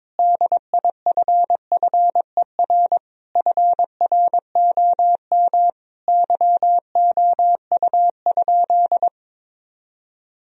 Morse code, heard as a keyed tone, about 21 wpm, 700 Hz. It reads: DIFFER FROM YOU?
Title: Sentences from Top 1000 Words - Review Part 1 - 22wpm